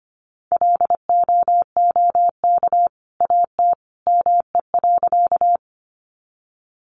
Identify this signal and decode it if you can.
Morse code, heard as a keyed tone, about 25 wpm, 700 Hz. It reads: LOOK AT ME.
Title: Sentences from Top 200 Words 25wpm